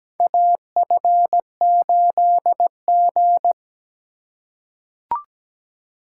Morse code, heard as a keyed tone, about 17 wpm, 700 Hz. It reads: AF8G
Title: USA Call Signs - 4 Characters - 2x Character Spacing - 17wpm